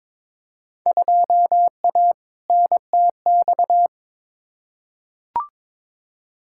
Morse code, heard as a keyed tone, about 22 wpm, 700 Hz. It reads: 2A NTX
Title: Field Day Exchanges - 22wpm